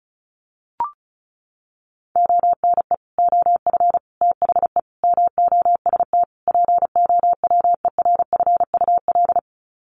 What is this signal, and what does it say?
Morse code, heard as a keyed tone, about 34 wpm, 700 Hz. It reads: ONE OF THE MOST POWERFUL
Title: Sets of 5 Words Encore 35wpm